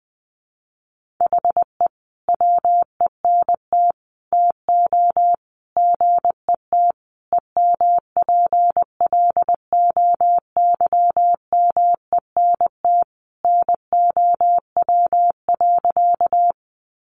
Morse code, heard as a keyed tone, about 20 wpm, 700 Hz. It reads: HE WENT TO GET EMPLOYMENT NOW.
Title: Sentences from Top 1500 Words 20wpm